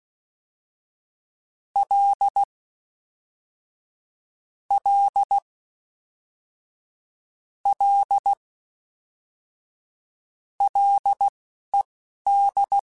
Morse code, heard as a keyed tone, about 16 wpm, spaced as for 11 wpm, 800 Hz. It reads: L L L LED
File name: F and L / F and L mp3 / F and L Prefixes 3x2 Twice @10